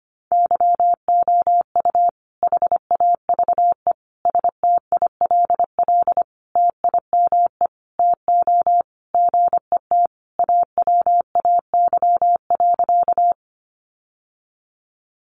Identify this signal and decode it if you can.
Morse code, heard as a keyed tone, about 25 wpm, 700 Hz. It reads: YOU HAVE STILL TIME TO GET AWAY.
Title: Sentences from Top 600 Words 25wpm